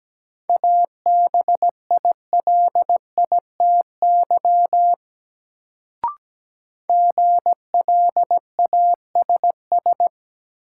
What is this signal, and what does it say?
Morse code, heard as a keyed tone, about 17 wpm, 700 Hz. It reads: ABILITY GLASS
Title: Sentences from Top 1000 Words - Review Part 3 - New Word Drill - 17wpm